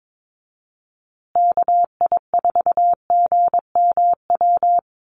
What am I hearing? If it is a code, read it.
Morse code, heard as a keyed tone, about 22 wpm, 700 Hz. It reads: KI4GMW
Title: USA Call Signs Repeated 3x 22wpm